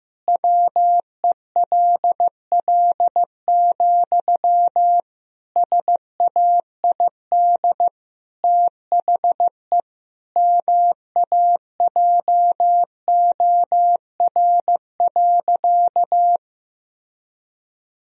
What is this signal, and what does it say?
Morse code, heard as a keyed tone, about 15 wpm, 700 Hz. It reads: WELL, SAID THE MAJOR.
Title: Sentences from Top 1000 Words - Review Part 4 - 15wpm